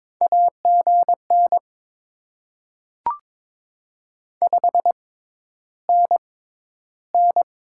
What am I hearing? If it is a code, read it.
Morse code, heard as a keyed tone, about 22 wpm, spaced as for 11 wpm, 690 Hz. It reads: AGN 5NN
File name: Top 20 QSO Elements - 6x Character Spacing - 22wpm